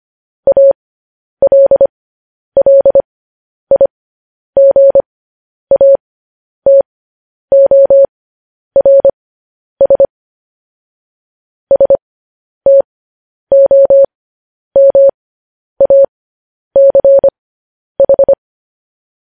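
Morse code, heard as a keyed tone, about 25 wpm, spaced as for 10 wpm, 550 Hz. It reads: ALLIGATORS STOMACH
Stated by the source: Morse code news headlines sent at 10.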